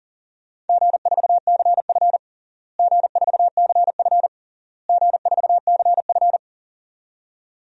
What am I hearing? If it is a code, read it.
Morse code, heard as a keyed tone, about 40 wpm, 700 Hz. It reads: G4CF G4CF G4CF
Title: DXCC-100 Call Signs - 4 Characters - Repeated 3x - 40wpm